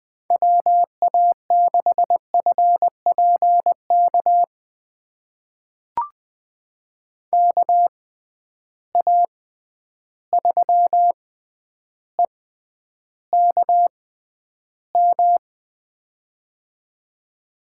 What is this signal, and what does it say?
Morse code, heard as a keyed tone, about 20 wpm, spaced as for 10 wpm, 700 Hz. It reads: WA6FPK KA3EKM
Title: USA Call Signs - 6 Characters - 6x Character Spacing - 20wpm